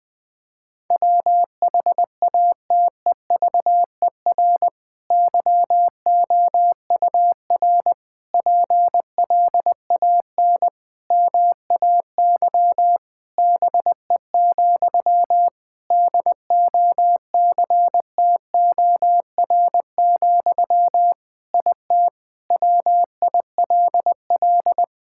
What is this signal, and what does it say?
Morse code, heard as a keyed tone, about 20 wpm, 700 Hz. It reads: WHATEVER YOUR PLAN MAY BE, DOCTOR, IT WILL
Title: Sentences from Top 1600 Words 20wpm